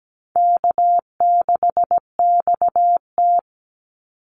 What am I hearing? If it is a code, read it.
Morse code, heard as a keyed tone, about 17 wpm, 700 Hz. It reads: K6XT
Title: Easy Call Signs Repeated 3x - 17wpm